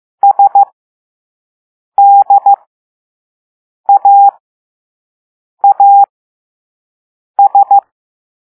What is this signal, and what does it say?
Morse code, heard as a keyed tone, about 15 wpm, spaced as for 5 wpm, 800 Hz. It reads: SDAAS
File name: Morse / Level3B